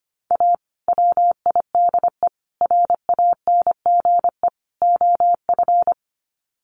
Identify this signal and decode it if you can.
Morse code, heard as a keyed tone, about 25 wpm, 700 Hz. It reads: A WIDE RANGE OF
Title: Sets of 4 Words 25wpm